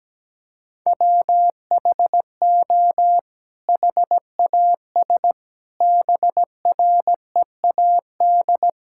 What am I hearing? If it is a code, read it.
Morse code, heard as a keyed tone, about 17 wpm, 700 Hz. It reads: WHO HAS BREAD
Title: Sentences from Top 1600 Words 17wpm